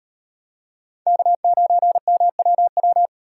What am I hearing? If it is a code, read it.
Morse code, heard as a keyed tone, about 36 wpm, 700 Hz. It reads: K9MWW